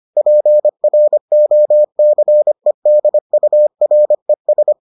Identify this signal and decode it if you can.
Morse code, heard as a keyed tone, about 25 wpm, 600 Hz. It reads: PROCEDURES